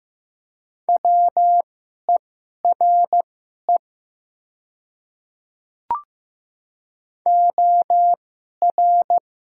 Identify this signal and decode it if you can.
Morse code, heard as a keyed tone, about 15 wpm, spaced as for 11 wpm, 700 Hz. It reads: WERE OR
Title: 2x Character Spacing - Top 200 Words - 15wpm